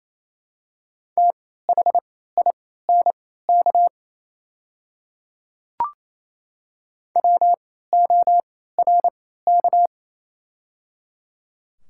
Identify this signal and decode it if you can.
Morse code, heard as a keyed tone, about 28 wpm, spaced as for 16 wpm, 700 Hz. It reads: THINK WORK
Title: 3x Character Spacing - Top 200 Words - 28wpm